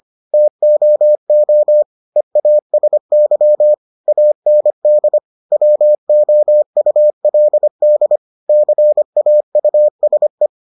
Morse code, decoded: TOO EASY AND WOULD CAUSE

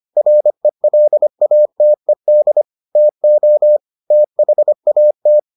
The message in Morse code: RELATED TO THAT

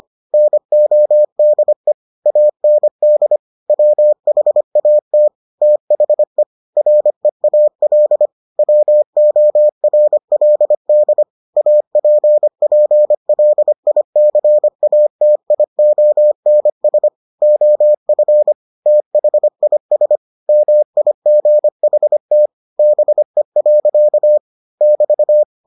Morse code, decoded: NODE AND WHAT THE REAL WORLD APPLICATIONS OF THIS MIGHT BE. <BT>

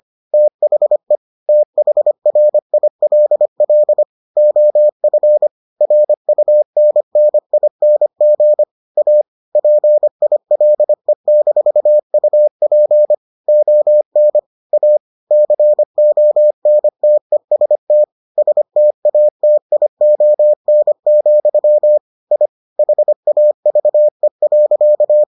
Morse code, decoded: THE THRILL OF RUNNING A PILE-UP ON A CONTEST STATION, I HAVE.

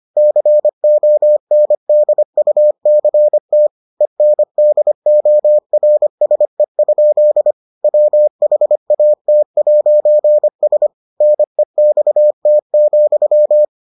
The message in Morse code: CONDUCT ENDORSE? WHAT'S NEXT,

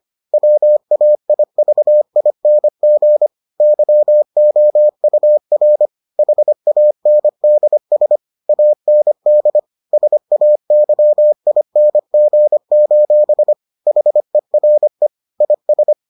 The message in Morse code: WAIVING YOUR HANDS AND SAYING: HERE IS

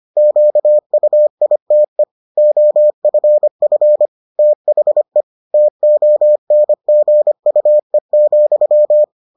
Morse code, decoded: QUITE OFF THE TONGUE,

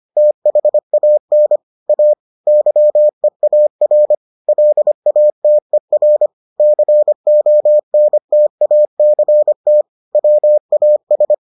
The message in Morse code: THAN A YEAR LATER CONTACT WAS

